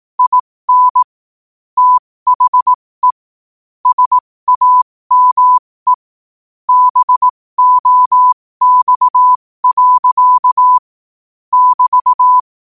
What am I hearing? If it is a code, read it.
Morse code, heard as a keyed tone, about 18 wpm, spaced as for 15 wpm, 1000 Hz. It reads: IN THE SAME BOX. <BT>
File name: qotdfiles / 15wpm